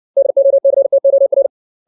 Morse code, heard as a keyed tone, about 60 wpm, 550 Hz. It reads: DOCTOR
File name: 632 words / 632 Words 60WPM